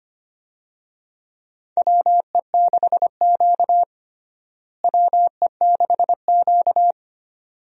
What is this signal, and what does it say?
Morse code, heard as a keyed tone, about 25 wpm, 700 Hz. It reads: WE6Q WE6Q
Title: USA Call Signs - 4 Characters - Repeated 2x - 25wpm